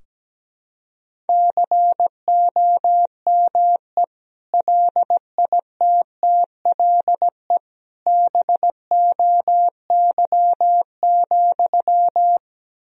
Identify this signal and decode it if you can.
Morse code, heard as a keyed tone, about 17 wpm, 700 Hz. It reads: COME LITTLE BOY,
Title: Sentences from Top 1000 Words - Review Part 2 - 17wpm